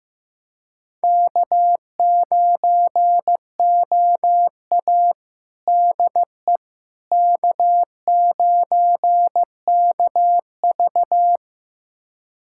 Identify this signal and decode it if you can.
Morse code, heard as a keyed tone, about 15 wpm, 700 Hz. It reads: K9OA DE K9KV
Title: USA Callsign DE Callsign - 4 Characters - 15wpm